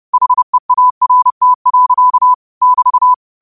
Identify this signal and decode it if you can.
Morse code, heard as a keyed tone, about 30 wpm, 1000 Hz. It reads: HEART. <BT>